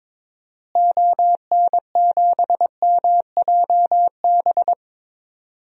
Morse code, heard as a keyed tone, about 22 wpm, 700 Hz. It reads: ON7MJB
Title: DXCC-100 Call Signs - 6 Characters - Repeated 3x - 22wpm